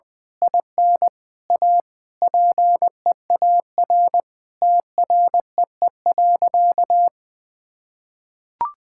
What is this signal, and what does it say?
Morse code, heard as a keyed tone, about 20 wpm, 700 Hz. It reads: IN A PEAR TREE.
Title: Sentences with 9-Word Phrases - 20wpm